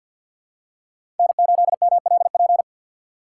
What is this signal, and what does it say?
Morse code, heard as a keyed tone, about 52 wpm, 700 Hz. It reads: N8MLP
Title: USA Call Signs - 5 Characters - 2x Character Spacing - 50wpm